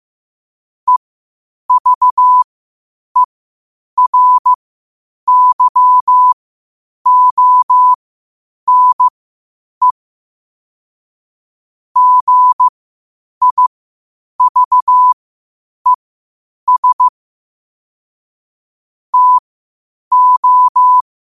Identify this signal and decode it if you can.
Morse code, heard as a keyed tone, about 15 wpm, spaced as for 8 wpm, 1000 Hz. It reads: EVERYONE GIVES TO
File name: qotdfiles / 8wpm